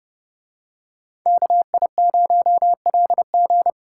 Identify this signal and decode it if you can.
Morse code, heard as a keyed tone, about 30 wpm, 700 Hz. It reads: KI0LG